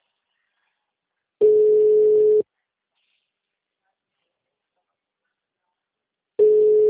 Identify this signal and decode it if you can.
Ringback tone (Europe, 425 Hz).